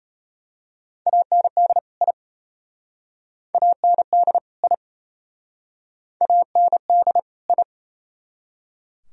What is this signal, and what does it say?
Morse code, heard as a keyed tone, about 32 wpm, 700 Hz. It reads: AND I AND I AND I
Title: Speed-Racing - Sets of 2 Words 28wpm